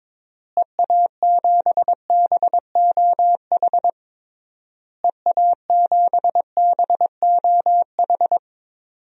Morse code, heard as a keyed tone, about 22 wpm, 700 Hz. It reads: EA7BOH EA7BOH
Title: DXCC-100 Call Signs - 6 Characters - Repeated 3x - 22wpm